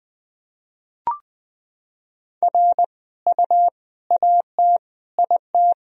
Morse code, heard as a keyed tone, about 19 wpm, 700 Hz. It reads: R U AT IT
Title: Sentences from Top 200 Words and Abbreviations 20wpm